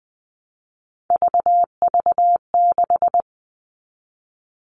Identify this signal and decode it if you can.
Morse code, heard as a keyed tone, about 20 wpm, 700 Hz. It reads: VV6
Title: All Cut Numbers - 3-Digit 20wpm